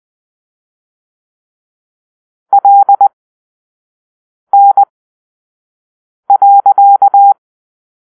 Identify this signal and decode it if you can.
Morse code, heard as a keyed tone, about 20 wpm, spaced as for 5 wpm, 800 Hz. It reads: LN.